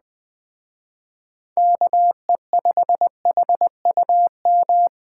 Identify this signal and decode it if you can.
Morse code, heard as a keyed tone, about 20 wpm, 700 Hz. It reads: KE5HUM
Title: Speed Racing - USA 2x3 Call Signs 17wpm